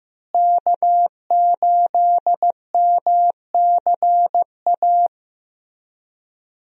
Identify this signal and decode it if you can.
Morse code, heard as a keyed tone, about 15 wpm, 700 Hz. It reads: K8MCA
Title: USA Call Signs - 5 Characters - Repeated 2x - 15wpm